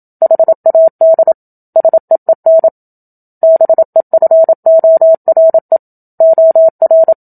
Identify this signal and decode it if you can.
Morse code, heard as a keyed tone, about 27 wpm, 650 Hz. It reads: HAD SEEN BEFORE OR
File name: Book Two / 01-Chapter 1 - 27 WPM